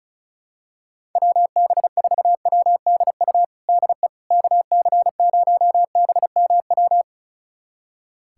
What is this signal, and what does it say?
Morse code, heard as a keyed tone, about 34 wpm, 700 Hz. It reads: WB4WDU DE KC0BMW
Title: USA Callsign DE Callsign - 6 Characters - 35wpm